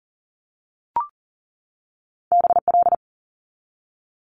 Morse code, heard as a keyed tone, about 38 wpm, 700 Hz. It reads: BL
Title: Multi-letter Phonemes 40wpm